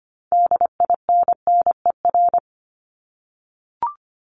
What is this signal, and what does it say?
Morse code, heard as a keyed tone, about 25 wpm, 700 Hz. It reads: DINNER
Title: Top 1751-2000 Words 25wpm